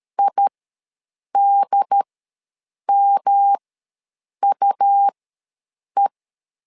Morse code, heard as a keyed tone, about 13 wpm, spaced as for 7 wpm, 800 Hz. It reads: IDMUE
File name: LEZIONI / 3 MWRO / 3 M ws 12wpm 1745